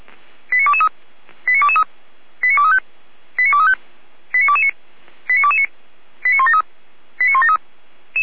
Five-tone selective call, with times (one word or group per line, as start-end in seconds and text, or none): 0.5-0.9 89223
1.5-1.8 89223
2.4-2.8 89235
3.4-3.8 89235
4.3-4.7 89229
5.3-5.7 89229
6.2-6.6 89173
7.2-7.6 89173